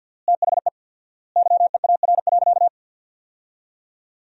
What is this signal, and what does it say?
Morse code, heard as a keyed tone, about 50 wpm, 700 Hz. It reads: THE YEAR.
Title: 4x Word-Spacing - Sentences from Top 300 Words - 50wpm